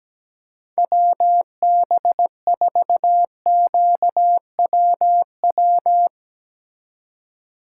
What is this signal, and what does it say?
Morse code, heard as a keyed tone, about 17 wpm, 700 Hz. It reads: WB4QWW
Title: Speed Racing - USA 2x3 Call Signs 17wpm